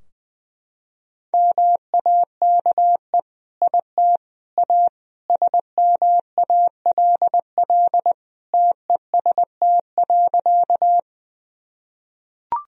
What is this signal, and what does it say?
Morse code, heard as a keyed tone, about 21 wpm, 700 Hz. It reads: MAKE IT A SMALL TEST.
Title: Sentences from Top 1000 Words - Review Part 3 - 20wpm